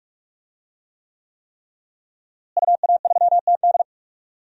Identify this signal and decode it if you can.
Morse code, heard as a keyed tone, about 41 wpm, 700 Hz. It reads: UA3TD
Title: DXCC-100 Call Signs - 5 Characters - Repeated 3x - 45wpm